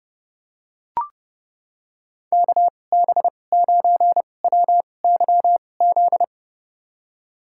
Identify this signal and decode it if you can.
Morse code, heard as a keyed tone, about 30 wpm, spaced as for 22 wpm, 700 Hz. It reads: KB9WYZ
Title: USA Call Signs - 2x Character Spacing - 30wpm